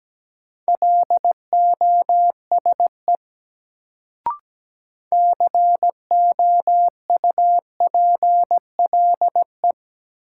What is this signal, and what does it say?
Morse code, heard as a keyed tone, about 17 wpm, 700 Hz. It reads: LOSE COUPLE
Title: Sentences from Top 1100 Words - New Word Drill - 17wpm